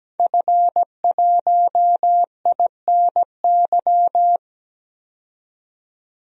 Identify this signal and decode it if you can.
Morse code, heard as a keyed tone, about 17 wpm, 700 Hz. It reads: F1INY
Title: DXCC-100 Call Signs - 5 Characters - 17wpm